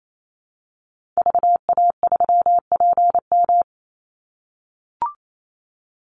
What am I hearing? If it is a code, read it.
Morse code, heard as a keyed tone, about 28 wpm, 700 Hz. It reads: VA3PM
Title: Call Signs - 28wpm